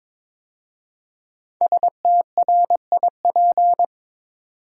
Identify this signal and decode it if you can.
Morse code, heard as a keyed tone, about 22 wpm, 700 Hz. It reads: STRIP
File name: 5 Letter Words - First 3 Letters Given - 22wpm